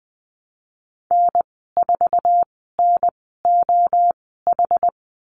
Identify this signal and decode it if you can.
Morse code, heard as a keyed tone, about 20 wpm, spaced as for 15 wpm, 700 Hz. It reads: N4NOH
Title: 231 Lesson 37g - Letter x - Call Signs 20wpm